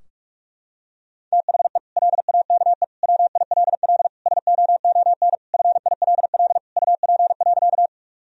Morse code, heard as a keyed tone, about 46 wpm, 700 Hz. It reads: THE LAKE WILL SOON FILL UP.
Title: Sentences from Top 1000 Words - Review Part 3 - 45wpm